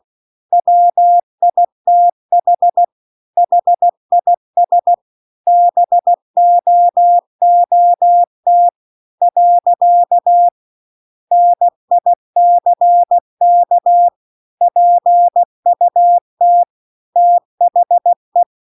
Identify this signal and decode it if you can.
Morse code, heard as a keyed tone, about 16 wpm, 700 Hz. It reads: WITH HIS BOOT . NICK PUT THE